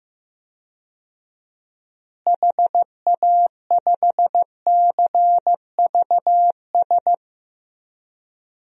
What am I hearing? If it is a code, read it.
Morse code, heard as a keyed tone, about 15 wpm, 700 Hz. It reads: HA5CVS
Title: DXCC-100 Call Signs - 6 Characters - Repeated 2x - 15wpm